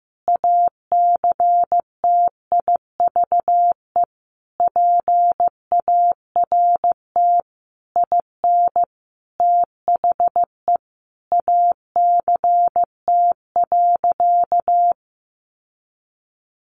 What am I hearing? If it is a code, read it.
Morse code, heard as a keyed tone, about 15 wpm, 700 Hz. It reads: ACTIVE PART IN THE ACT.
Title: Sentences from Top 1300 Words 15wpm